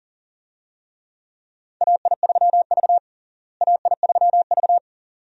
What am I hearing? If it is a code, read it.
Morse code, heard as a keyed tone, about 40 wpm, 700 Hz. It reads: AI3V AI3V
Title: USA Call Signs - 4 Characters - Repeated 2x - 40wpm